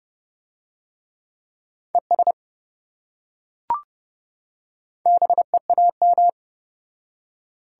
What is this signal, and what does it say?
Morse code, heard as a keyed tone, about 27 wpm, 700 Hz. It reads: ES BEAM